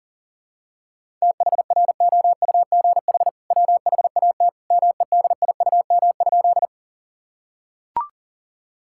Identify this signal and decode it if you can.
Morse code, heard as a keyed tone, about 40 wpm, 700 Hz. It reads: THROUGH WHAT MEDIUM?
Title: Sentences from Top 1000 Words - Review Part 6 - 40wpm